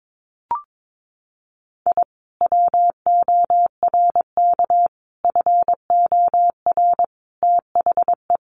Morse code, heard as a keyed tone, about 22 wpm, 700 Hz. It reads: I WORK FOR THE
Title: Sentences from Top 600 Words 22wpm